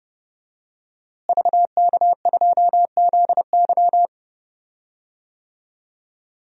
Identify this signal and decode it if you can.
Morse code, heard as a keyed tone, about 32 wpm, 700 Hz. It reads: VK2ZY
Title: DXCC-100 Call Signs - 5 Characters - Repeated 2x - 30wpm